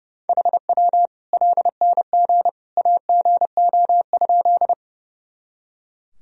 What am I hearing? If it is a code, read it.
Morse code, heard as a keyed tone, about 32 wpm, 700 Hz. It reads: HW LNG AGO?
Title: Sentences from Top 500 Words and Abbreviations 30wpm